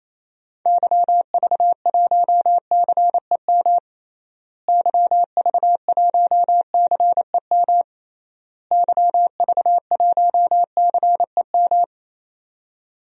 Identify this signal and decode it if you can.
Morse code, heard as a keyed tone, about 28 wpm, 700 Hz. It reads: YV1CEM YV1CEM YV1CEM
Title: DXCC-100 Call Signs - 6 Characters - Repeated 3x - 28wpm